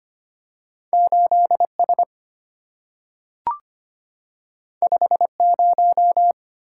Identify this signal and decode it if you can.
Morse code, heard as a keyed tone, about 25 wpm, 700 Hz. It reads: 8S 50
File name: Two Letter-Number Combinations - 25wpm